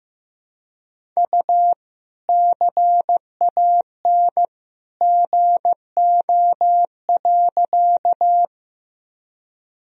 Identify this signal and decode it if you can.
Morse code, heard as a keyed tone, about 15 wpm, 700 Hz. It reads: U CAN GO.
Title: Sentences from Top 100 Words and Abbreviations 15wpm